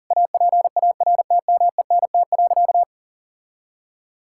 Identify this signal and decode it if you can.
Morse code, heard as a keyed tone, about 37 wpm, 700 Hz. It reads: APARTMENT.
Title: Sentences from Top 1000 Words - Review Part 6 - 40wpm